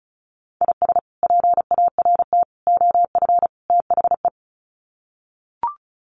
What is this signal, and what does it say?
Morse code, heard as a keyed tone, about 34 wpm, 700 Hz. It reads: IS PART OF THE
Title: Sets of 4 Words 35wpm